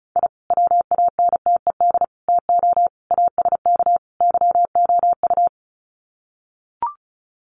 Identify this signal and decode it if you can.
Morse code, heard as a keyed tone, about 34 wpm, 700 Hz. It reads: I WANTED TO ASK YOU
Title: Sets of 5 Words Encore2 35wpm